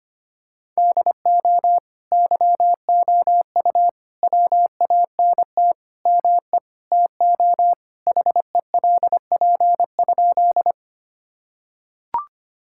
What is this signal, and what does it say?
Morse code, heard as a keyed tone, about 25 wpm, 700 Hz. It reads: DO YOU WANT ME TO HELP?
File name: practice / 25wpm / Sentences with 4-Word Phrases - 25wpm